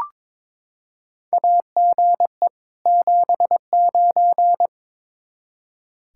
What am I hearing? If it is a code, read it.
Morse code, heard as a keyed tone, about 21 wpm, 700 Hz. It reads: AGE 79
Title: Practice QSO - Exchange 3 - More Info - 22wpm